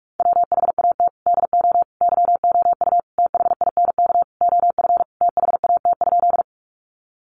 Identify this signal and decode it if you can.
Morse code, heard as a keyed tone, about 44 wpm, 700 Hz. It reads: WHAT DO YOU THINK OF THAT?
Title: Sentences from Top 200 Words 45wpm